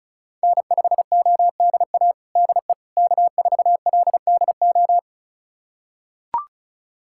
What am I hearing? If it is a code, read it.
Morse code, heard as a keyed tone, about 34 wpm, 700 Hz. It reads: N5ODA DE K4LDO
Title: USA Callsign DE Callsign - 5 Characters - 35wpm